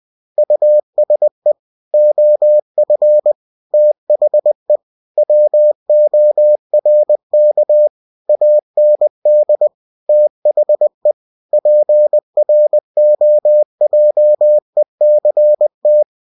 Morse code, heard as a keyed tone, about 20 wpm, 600 Hz. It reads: USE OF THE WORK AND THE PROJECT
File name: cw-books / winnie-the-pooh / winnie-20wpm-page-0115